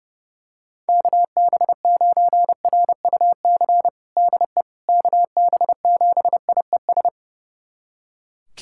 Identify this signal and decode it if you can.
Morse code, heard as a keyed tone, about 28 wpm, 700 Hz. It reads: KB9RUC DE KB7IES
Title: USA Callsign DE Callsign - 6 Characters - 30wpm